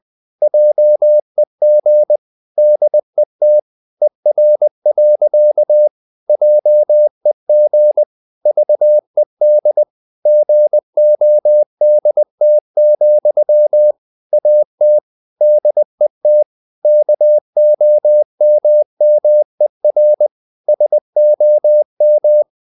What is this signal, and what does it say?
Morse code, heard as a keyed tone, about 20 wpm, 600 Hz. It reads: JEG DET ER. JEG VED GODT, AT DET KOMMER SOM
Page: Ferie 20 wpm